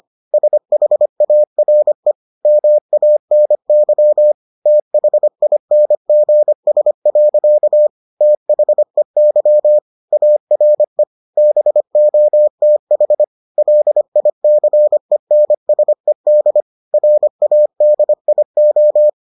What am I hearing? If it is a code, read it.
Morse code, heard as a keyed tone, about 25 wpm, 600 Hz. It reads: SHARE MANY THINGS. THEY ARE BOTH LICENSED RADIO